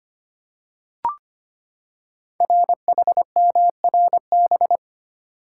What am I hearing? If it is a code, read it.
Morse code, heard as a keyed tone, about 25 wpm, 700 Hz. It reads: RHMRB
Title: Code Groups - 5 Letters - 25wpm